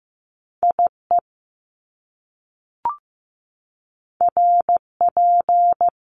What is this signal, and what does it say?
Morse code, heard as a keyed tone, about 15 wpm, 700 Hz. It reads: IE RP
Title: Top Two Letter Combinations in Words 15wpm